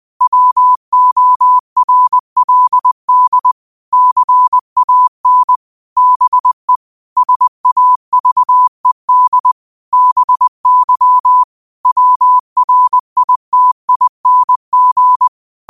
Morse code, heard as a keyed tone, about 20 wpm, 1000 Hz. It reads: WORLD CAN BE SAVED BY WRITING
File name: qotdfiles / 20wpm